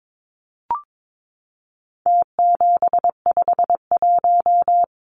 Morse code, heard as a keyed tone, about 22 wpm, 700 Hz. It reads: T751
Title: Numbers - Contest Style - 4-Digit 22wpm